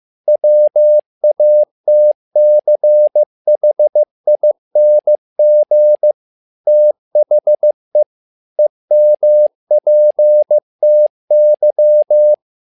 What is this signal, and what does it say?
Morse code, heard as a keyed tone, about 15 wpm, 600 Hz. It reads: WATCHING THE EMPTY
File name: learning center / tower trouble / tower trouble 15wpm